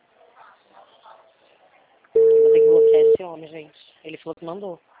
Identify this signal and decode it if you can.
Ringback tone (China, 450 Hz), with speech.